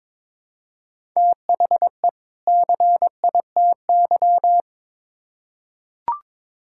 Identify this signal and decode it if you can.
Morse code, heard as a keyed tone, about 21 wpm, 700 Hz. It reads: THE CITY
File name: Build A Phrase - 2-Word Phrases - 22wpm